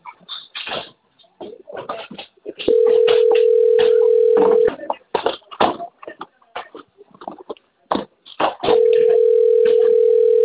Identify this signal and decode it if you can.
Ringback tone (North America), 440 + 480 Hz.